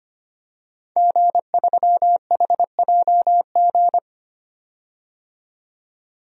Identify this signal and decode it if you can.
Morse code, heard as a keyed tone, about 25 wpm, 700 Hz. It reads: G3HJG
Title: DXCC-100 Call Signs - 5 Characters - 25wpm